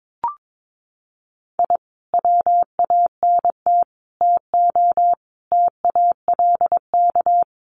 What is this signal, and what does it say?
Morse code, heard as a keyed tone, about 22 wpm, 700 Hz. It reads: I WANT TO TALK
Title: Sets of 5 Words Encore 22wpm